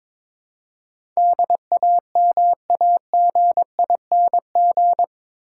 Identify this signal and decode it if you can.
Morse code, heard as a keyed tone, about 22 wpm, 700 Hz. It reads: DAMAGING
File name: practice / 22wpm / Eight Letter Words - 22wpm